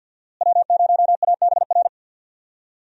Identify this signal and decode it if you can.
Morse code, heard as a keyed tone, about 51 wpm, 700 Hz. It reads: W0ABR
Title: USA Call Signs - 5 Characters - Repeated 3x - 50wpm